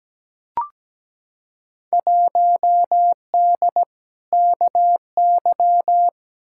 Morse code, heard as a keyed tone, about 17 wpm, 700 Hz. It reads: 1D KY
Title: Field Day Exchanges - 17wpm